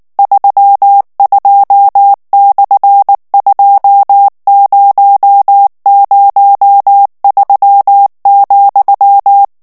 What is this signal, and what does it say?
Morse code, heard as a keyed tone, about 19 wpm, 800 Hz. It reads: 32/2003,
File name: ESPAGNOL TEXTE EN CLAIR 2 VITESSE 18 MOTS MINUTES